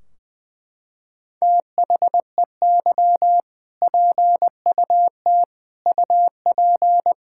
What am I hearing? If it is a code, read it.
Morse code, heard as a keyed tone, about 20 wpm, 700 Hz. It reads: THEY PUT UP
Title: Sentences from Top 1600 Words 20wpm